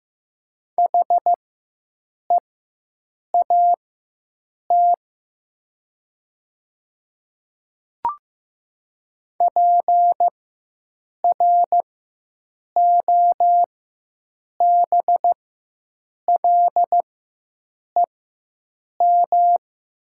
Morse code, heard as a keyed tone, about 15 wpm, spaced as for 7 wpm, 700 Hz. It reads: HEAT PROBLEM
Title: 4x Character Spacing - Top 500 Words - 15wpm